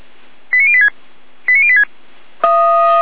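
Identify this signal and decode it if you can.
Five-tone selective call (ZVEI), heard as quick runs of five tones, tones of about 70 ms. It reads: 89086 89086
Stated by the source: ZVEI five-tone call 89080 received on an SDR.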